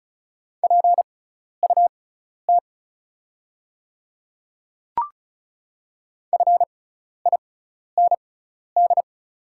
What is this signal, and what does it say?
Morse code, heard as a keyed tone, about 36 wpm, spaced as for 12 wpm, 690 Hz. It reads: PUT FIND
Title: 6x Character Spacing - Top 200 Words - 35wpm